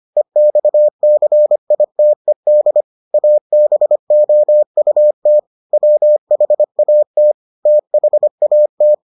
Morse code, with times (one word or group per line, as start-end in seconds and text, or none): 0.2-2.8 EXCITED
3.1-5.4 ABOUT
5.7-7.3 WHAT
7.7-8.9 THAT